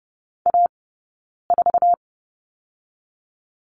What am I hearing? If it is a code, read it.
Morse code, heard as a keyed tone, about 31 wpm, 700 Hz. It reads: A 4
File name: Lesson 16d - Letter c - Two Characters 30wpm